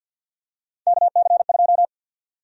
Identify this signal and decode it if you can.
Morse code, heard as a keyed tone, about 49 wpm, 700 Hz. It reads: KC2
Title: USA Call Sign Prefixes - 50wpm